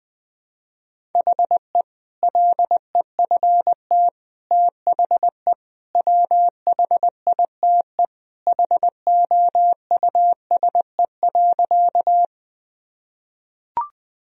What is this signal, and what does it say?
Morse code, heard as a keyed tone, about 20 wpm, 700 Hz. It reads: HE LEFT THE WHITE HOUSE.
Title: Sentences from Top 1000 Words - Review Part 1 - 20wpm